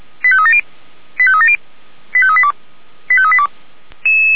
Five-tone selective call (ZVEI), heard as quick runs of five tones, tones of about 70 ms. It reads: 85370 85370 85382 85382